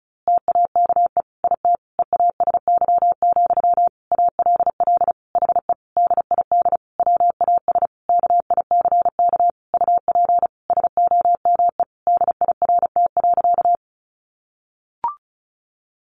Morse code, heard as a keyed tone, about 34 wpm, 700 Hz. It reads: TAKE IT EASY, ALL HE DID WAS KICK UP SOME DIRT.
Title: A Sound Of Thunder 35wpm